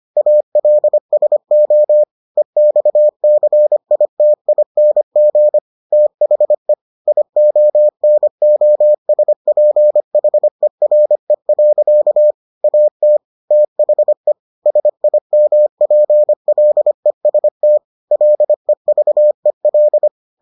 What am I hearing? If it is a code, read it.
Morse code, heard as a keyed tone, about 25 wpm, 600 Hz. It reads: ALSO EXCITING THE IONOSPHERE. AT THE SIMPLEST LEVEL